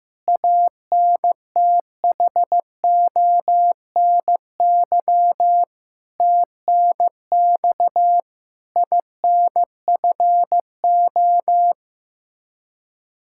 Morse code, heard as a keyed tone, about 15 wpm, 700 Hz. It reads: ANTHONY TNX INFO
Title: Practice QSO - Exchange 3 - More Info - 15wpm